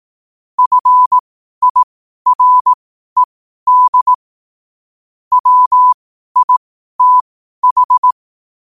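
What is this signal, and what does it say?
Morse code, heard as a keyed tone, about 18 wpm, spaced as for 12 wpm, 1000 Hz. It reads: FIRED WITH
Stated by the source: Morse code at 12 wpm.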